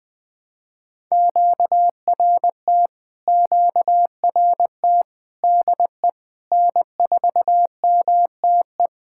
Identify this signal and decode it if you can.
Morse code, heard as a keyed tone, about 20 wpm, 700 Hz. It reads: QRT QRT DE N4MTE
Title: POTA QSO Practice - 20wpm